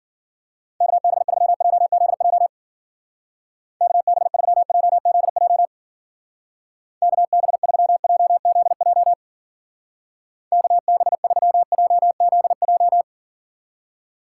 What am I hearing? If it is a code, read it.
Morse code, heard as a keyed tone, about 47 wpm, 700 Hz. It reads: KB3JZJ KB3JZJ KB3JZJ KB3JZJ
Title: Speed Racing - USA 2x3 Call Signs 40wpm